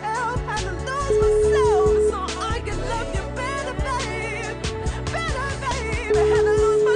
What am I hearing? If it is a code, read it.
Ringback tone (Europe, 425 Hz), with noise.